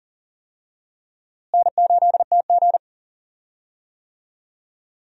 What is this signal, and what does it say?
Morse code, heard as a keyed tone, about 40 wpm, 700 Hz. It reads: N8TG
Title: USA Call Signs - 4 Characters - Repeated 3x - 40wpm